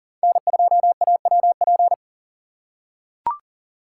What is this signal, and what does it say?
Morse code, heard as a keyed tone, about 40 wpm, 700 Hz. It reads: N2AWP
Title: USA Call Signs - 6x Character Spacing - 40wpm